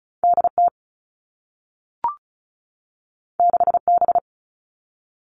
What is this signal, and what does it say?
Morse code, heard as a keyed tone, about 36 wpm, 700 Hz. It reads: DT 6B
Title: All Cut Numbers - 2-Digit 35wpm